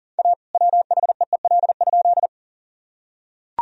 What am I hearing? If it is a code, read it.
Morse code, heard as a keyed tone, about 38 wpm, 700 Hz. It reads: A WHEEL?
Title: Sentences from Top 1000 Words - Review Part 3 - 40wpm